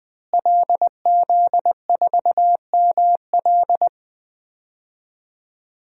Morse code, heard as a keyed tone, about 20 wpm, 700 Hz. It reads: LZ4ML
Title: DXCC-100 Call Signs - 5 Characters - Repeated 2x - 20wpm